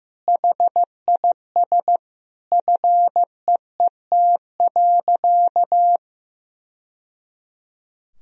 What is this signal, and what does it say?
Morse code, heard as a keyed tone, about 15 wpm, 700 Hz. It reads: HIS FEET.
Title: Sentences from Top 1000 Words - Review Part 2 - 15wpm